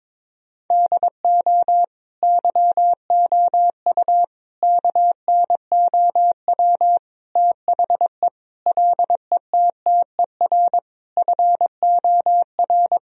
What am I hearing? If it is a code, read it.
Morse code, heard as a keyed tone, about 22 wpm, 700 Hz. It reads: DO YOU KNOW THE LETTER FOR
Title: Sentences from Top 1000 Words - Review Part 2 - 22wpm